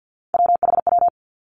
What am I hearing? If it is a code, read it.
Morse code, heard as a keyed tone, about 50 wpm, 700 Hz. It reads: WHW
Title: Three Challenging Letter-Number Combinations 50wpm